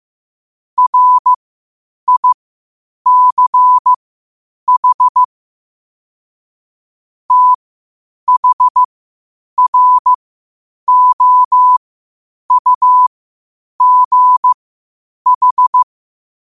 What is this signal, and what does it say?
Morse code, heard as a keyed tone, about 15 wpm, spaced as for 8 wpm, 1000 Hz. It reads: RICH THROUGH
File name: qotdfiles / 8wpm